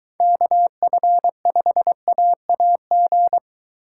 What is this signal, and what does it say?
Morse code, heard as a keyed tone, about 23 wpm, 700 Hz. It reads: KF5AAG